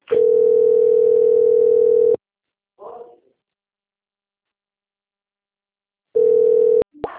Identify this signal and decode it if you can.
Ringback tone (North America), 440 + 480 Hz.